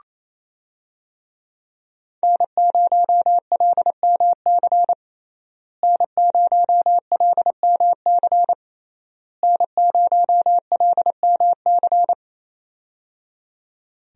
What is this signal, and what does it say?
Morse code, heard as a keyed tone, about 27 wpm, 700 Hz. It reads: N0LMC N0LMC N0LMC
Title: USA Call Signs - 5 Characters - Repeated 3x - 28wpm